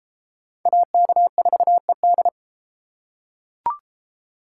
Morse code, heard as a keyed tone, about 33 wpm, 700 Hz. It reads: AK4ED